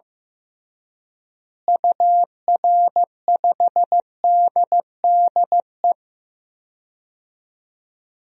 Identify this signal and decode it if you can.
Morse code, heard as a keyed tone, about 16 wpm, 700 Hz. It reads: UR5DDE
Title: DXCC-100 Call Signs - 6 Characters - Repeated 3x - 15wpm